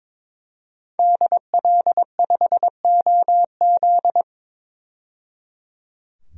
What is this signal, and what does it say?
Morse code, heard as a keyed tone, about 22 wpm, 700 Hz. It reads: DL5OZ
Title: DXCC-100 Call Signs - 5 Characters - 22wpm